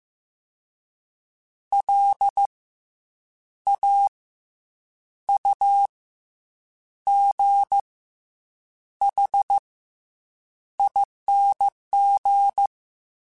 Morse code, heard as a keyed tone, about 15 wpm, 800 Hz. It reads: L A U G H ING
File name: Special INGs / Special INGs-mp3s / SlowToFaster ing @13